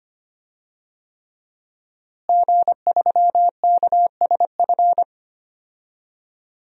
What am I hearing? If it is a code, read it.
Morse code, heard as a keyed tone, about 25 wpm, 700 Hz. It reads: G3KSF
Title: DXCC-100 Call Signs - 5 Characters - Repeated 3x - 25wpm